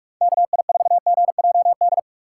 Morse code, heard as a keyed tone, about 43 wpm, 700 Hz. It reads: KI4GJD
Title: USA Call Signs - 6 Characters - 4x Character Spacing - 45wpm